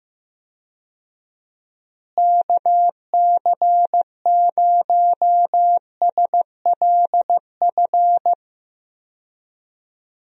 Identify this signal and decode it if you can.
Morse code, heard as a keyed tone, about 15 wpm, 700 Hz. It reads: KC0SLF
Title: USA Call Signs - 6 Characters - Repeated 3x - 15wpm